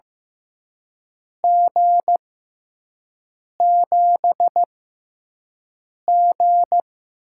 Morse code, heard as a keyed tone, about 15 wpm, spaced as for 5 wpm, 700 Hz. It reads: G7G
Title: USA Call Signs - 4 Characters - 6x Character Spacing - 15wpm